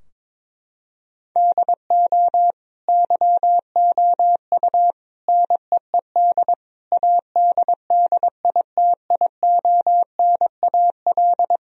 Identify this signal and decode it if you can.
Morse code, heard as a keyed tone, about 22 wpm, 700 Hz. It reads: DO YOU NEED ADDITIONAL
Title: Sentences from Top 1000 Words - Review Part 4 - 22wpm